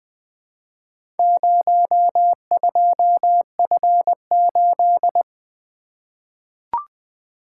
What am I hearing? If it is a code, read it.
Morse code, heard as a keyed tone, about 20 wpm, 700 Hz. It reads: 02F8
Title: Four Letter-Number Combinations - 20wpm